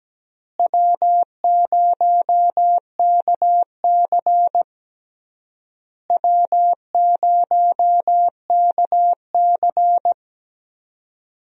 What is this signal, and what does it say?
Morse code, heard as a keyed tone, about 17 wpm, 700 Hz. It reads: W0KC W0KC
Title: USA Call Signs - 4 Characters - Repeated 3x - 17wpm